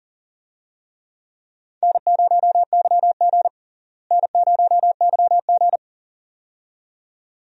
Morse code, heard as a keyed tone, about 40 wpm, 700 Hz. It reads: N0YG N0YG
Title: USA Call Signs - 4 Characters - Repeated 2x - 40wpm